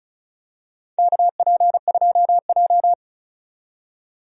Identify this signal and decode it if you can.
Morse code, heard as a keyed tone, about 36 wpm, 700 Hz. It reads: KP2J
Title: DXCC-100 Call Signs - 4 Characters - 35wpm